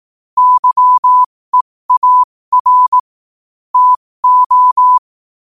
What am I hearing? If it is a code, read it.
Morse code, heard as a keyed tone, about 18 wpm, spaced as for 15 wpm, 1000 Hz. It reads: YEAR TO
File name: qotdfiles / 15wpm